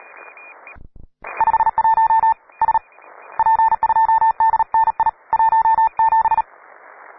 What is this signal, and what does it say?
Morse code, heard as a keyed tone, about 36 wpm, spaced as for 28 wpm, 890 Hz. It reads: 51 SP2DNI 17
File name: wae / cw pro / rt9w cw